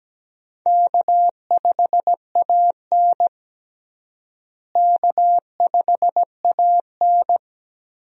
Morse code, heard as a keyed tone, about 17 wpm, 700 Hz. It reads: K5AN K5AN
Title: USA Call Signs - 4 Characters - Repeated 2x - 17wpm